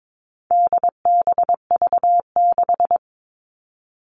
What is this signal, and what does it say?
Morse code, heard as a keyed tone, about 22 wpm, 700 Hz. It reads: DBV6
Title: All Cut Numbers - 4-Digit 22wpm